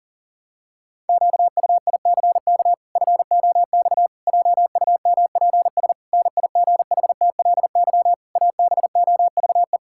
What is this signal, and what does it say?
Morse code, heard as a keyed tone, about 40 wpm, 700 Hz. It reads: QUICK FOX JUMPS NIGHTLY ABOVE
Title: Pangram Challenge v2 - 40wpm